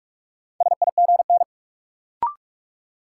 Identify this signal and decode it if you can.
Morse code, heard as a keyed tone, about 46 wpm, 700 Hz. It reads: SIGN
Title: Sentences from Top 1000 Words - New Word Drill - 45wpm